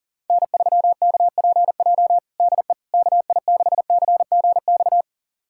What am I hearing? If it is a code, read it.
Morse code, heard as a keyed tone, about 40 wpm, 700 Hz. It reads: N3KPJ DE KI6CGX
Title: USA Callsign DE Callsign 40wpm